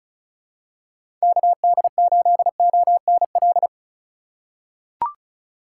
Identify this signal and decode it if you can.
Morse code, heard as a keyed tone, about 36 wpm, 700 Hz. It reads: KD8ONL
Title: USA Call Signs - 6 Characters - 2x Character Spacing - 35wpm